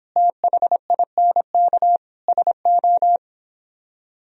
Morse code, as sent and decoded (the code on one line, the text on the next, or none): - .... .. -. -.- / ... ---
THINK SO